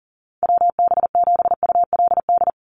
Morse code, heard as a keyed tone, about 40 wpm, 700 Hz. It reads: WB7ULD
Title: NA SOTA Chaser Callsigns 40wpm